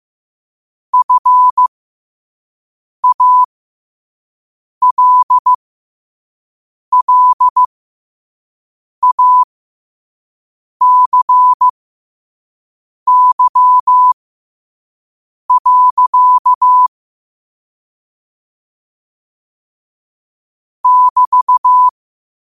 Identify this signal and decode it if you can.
Morse code, heard as a keyed tone, about 15 wpm, spaced as for 5 wpm, 1000 Hz. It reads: FALLACY. <BT>
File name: qotdfiles / 5wpm